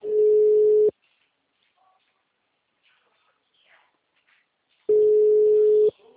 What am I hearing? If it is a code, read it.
Ringback tone (Europe, 425 Hz), with speech.